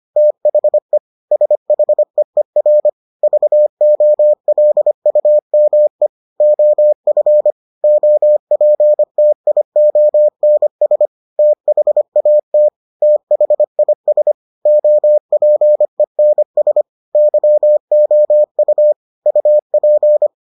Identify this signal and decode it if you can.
Morse code, heard as a keyed tone, about 25 wpm, 600 Hz. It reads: THE SHEER VOLUME OF OPTIONS THAT THIS OPENS YOU UP